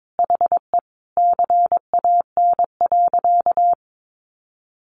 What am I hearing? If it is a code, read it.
Morse code, heard as a keyed tone, about 22 wpm, 700 Hz. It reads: HE CAN.
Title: Sentences from Top 1500 Words 22wpm